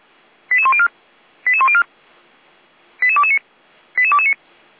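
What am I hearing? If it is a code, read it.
Five-tone selective call (ZVEI), heard as quick runs of five tones, tones of about 70 ms. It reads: 88104 88104 88229 88229
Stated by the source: SDR-received ZVEI five-tone call 88102.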